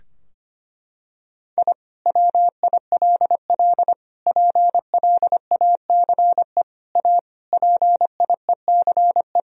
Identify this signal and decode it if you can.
Morse code, heard as a keyed tone, about 25 wpm, 700 Hz. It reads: I WILL PLACE A PIECE
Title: Sentences from Top 1000 Words - Review Part 2 - 25wpm